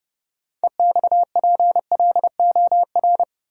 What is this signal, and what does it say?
Morse code, heard as a keyed tone, about 30 wpm, 700 Hz. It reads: EXPLOR
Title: Sentences from Top 1100 Words - New Word Drill - 30wpm